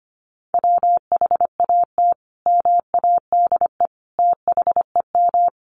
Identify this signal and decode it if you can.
Morse code, heard as a keyed tone, about 25 wpm, 700 Hz. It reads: WHAT MADE THEM
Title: Sentences from Top 1300 Words 25wpm